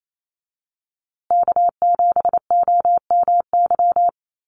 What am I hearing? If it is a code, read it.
Morse code, heard as a keyed tone, about 28 wpm, 700 Hz. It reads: K7OMY
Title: USA Call Signs Repeated 3x 28wpm